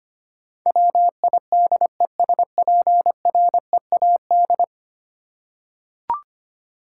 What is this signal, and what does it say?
Morse code, heard as a keyed tone, about 25 wpm, 700 Hz. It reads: WIDESPREAD
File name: Ten Letter Words - 25wpm